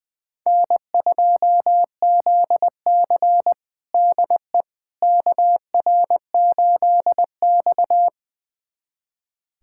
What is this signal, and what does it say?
Morse code, heard as a keyed tone, about 20 wpm, 700 Hz. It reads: N2ZC DE KR8X
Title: USA Callsign DE Callsign - 4 Characters - 20wpm